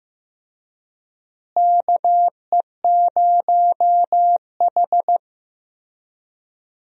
Morse code, heard as a keyed tone, about 15 wpm, 700 Hz. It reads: KE0H
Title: USA Call Signs - 4 Characters - Repeated 2x - 15wpm